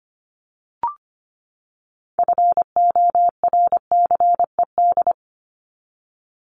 Morse code, heard as a keyed tone, about 25 wpm, 700 Hz. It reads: FORCED
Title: ED Suffix - Top 100 Words with Base Words 25wpm